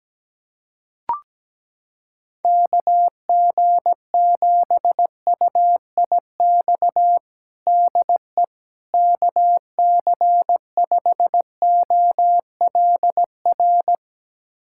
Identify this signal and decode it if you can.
Morse code, heard as a keyed tone, about 17 wpm, 700 Hz. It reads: KG7UIX DE KC5OLR
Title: USA Callsign DE Callsign - 6 Characters - 17wpm